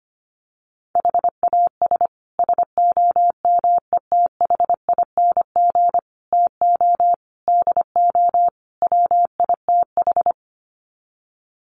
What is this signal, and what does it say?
Morse code, heard as a keyed tone, about 25 wpm, 700 Hz. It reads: HAS SOMETHING TO DO WITH
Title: Sets of 5 Words Encore 25wpm